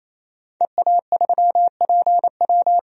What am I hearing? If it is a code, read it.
Morse code, heard as a keyed tone, about 28 wpm, 700 Hz. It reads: EA3PW
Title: DXCC-100 Call Signs - 5 Characters - 28wpm